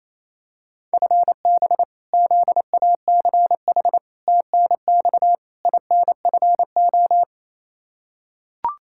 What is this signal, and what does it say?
Morse code, heard as a keyed tone, about 27 wpm, 700 Hz. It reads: FB ZACH TNX INFO
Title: K1USN SST QSO Practice 28wpm